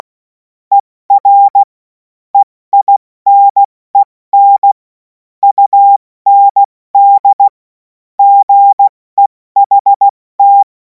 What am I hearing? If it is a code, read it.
Morse code, heard as a keyed tone, about 16 wpm, 800 Hz. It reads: ER EINEN UND GEHT
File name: Morsekurs / Witze / Witz 17. März 2026 70 F80